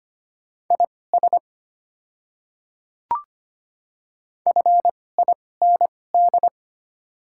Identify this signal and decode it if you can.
Morse code, heard as a keyed tone, about 25 wpm, spaced as for 18 wpm, 700 Hz. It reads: IS FIND
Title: 2x Character Spacing - Top 100 Words - 25wpm